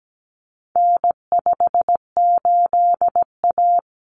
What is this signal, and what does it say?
Morse code, heard as a keyed tone, about 17 wpm, 700 Hz. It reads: N58A
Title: Common Cut Numbers - 4-Digit 17wpm